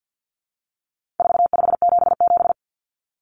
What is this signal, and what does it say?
Morse code, heard as a keyed tone, about 50 wpm, 700 Hz. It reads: 4577
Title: Common Cut Numbers - 4-Digit 50wpm